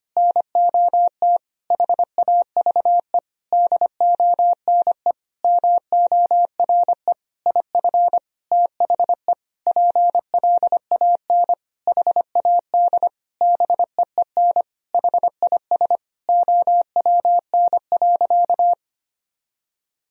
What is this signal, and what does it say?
Morse code, heard as a keyed tone, about 25 wpm, 700 Hz. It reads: NOT HAVE DONE MORE IF THE PLAN HAD BEEN HIS OWN.
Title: Sentences from Top 1000 Words - Review Part 2 - 25wpm